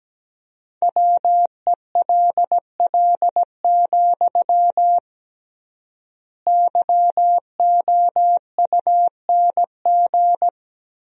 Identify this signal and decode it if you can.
Morse code, heard as a keyed tone, about 17 wpm, 700 Hz. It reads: WELL, YOUNG
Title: 3x Word-Spacing - Sentences from Top 400 Words - 17wpm